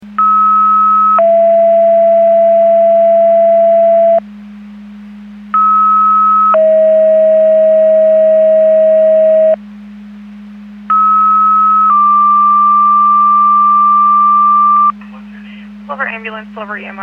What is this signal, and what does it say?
Two-tone page, heard as a one-second tone, then a three-second tone, about 1290 then 670 Hz, about 1290 then 640 Hz, about 1290 then 1180 Hz, with speech. It is heard from 0.2 to 14.9 s.